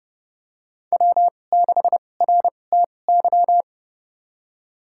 Morse code, heard as a keyed tone, about 30 wpm, spaced as for 22 wpm, 700 Hz. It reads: W6RTY
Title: USA Call Signs - 2x Character Spacing - 30wpm